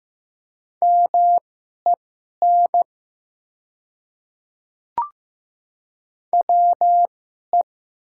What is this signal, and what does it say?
Morse code, heard as a keyed tone, about 15 wpm, spaced as for 11 wpm, 700 Hz. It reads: MEN WE
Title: 2x Character Spacing - Top 200 Words - 15wpm